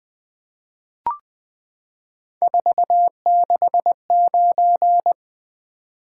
Morse code, heard as a keyed tone, about 20 wpm, 700 Hz. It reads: 469
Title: Three Letter-Number Combinations - 20wpm